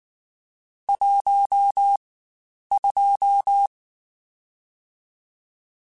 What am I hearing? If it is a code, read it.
Morse code, heard as a keyed tone, about 19 wpm, 800 Hz. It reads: 1 2
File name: Spaced Characters / Spaced Characters-mp3s / Weather5 22 sp @18